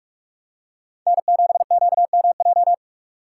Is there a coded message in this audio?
Yes: Morse code from 1.1 to 2.7 s, heard as a keyed tone, about 43 wpm, 700 Hz. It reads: N7QMJ